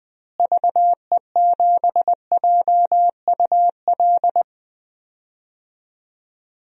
Morse code, heard as a keyed tone, about 20 wpm, 700 Hz. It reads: VE7JUL
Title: DXCC-100 Call Signs - 6 Characters - Repeated 3x - 20wpm